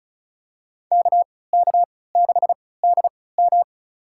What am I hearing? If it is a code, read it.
Morse code, heard as a keyed tone, about 34 wpm, spaced as for 20 wpm, 700 Hz. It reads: KK6DM
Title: USA Call Signs - 5 Characters - 3x Character Spacing - 35wpm